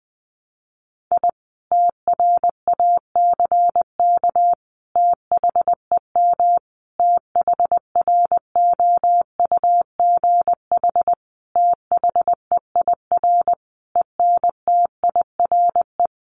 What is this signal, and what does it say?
Morse code, heard as a keyed tone, about 20 wpm, 700 Hz. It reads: I TRACK THEM THROUGH THEIR ENTIRE
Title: A Sound Of Thunder 20wpm